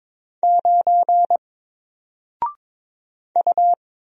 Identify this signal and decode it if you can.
Morse code, heard as a keyed tone, about 22 wpm, 700 Hz. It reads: 9 U
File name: Single-Letter-Number-22wpm